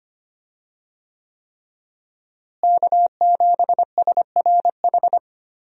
Morse code, heard as a keyed tone, about 25 wpm, 700 Hz. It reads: K7SRH
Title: USA Call Signs - 5 Characters - Repeated 3x - 25wpm